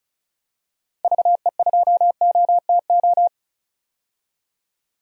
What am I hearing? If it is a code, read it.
Morse code, heard as a keyed tone, about 34 wpm, 700 Hz. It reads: VE2OTO
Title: DXCC-100 Call Signs - 6 Characters - 35wpm